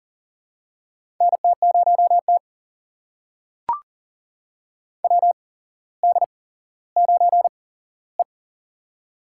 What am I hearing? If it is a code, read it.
Morse code, heard as a keyed tone, about 40 wpm, spaced as for 17 wpm, 700 Hz. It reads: NT0T WD9E
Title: USA Call Signs - 8x Character Spacing - 40wpm